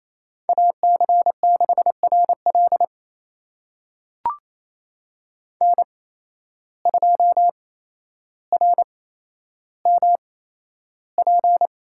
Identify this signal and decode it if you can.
Morse code, heard as a keyed tone, about 28 wpm, spaced as for 12 wpm, 700 Hz. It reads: AC6RL N2RMP
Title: USA Call Signs - 5 Characters - 8x Character Spacing - 28wpm